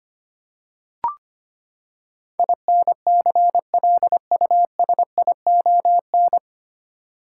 Morse code, heard as a keyed tone, about 25 wpm, 700 Hz. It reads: INCLUSION
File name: Nine Letter Words - 25wpm